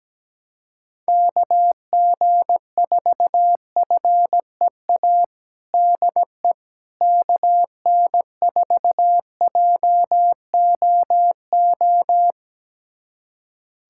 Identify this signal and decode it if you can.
Morse code, heard as a keyed tone, about 17 wpm, 700 Hz. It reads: KG4FEA DE KN4JOO
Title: USA Callsign DE Callsign - 6 Characters - 17wpm